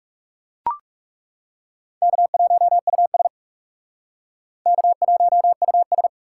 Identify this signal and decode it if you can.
Morse code, heard as a keyed tone, about 37 wpm, 700 Hz. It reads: K1US K1US
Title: Speed Racing - USA 1x2 Call Signs 30wpm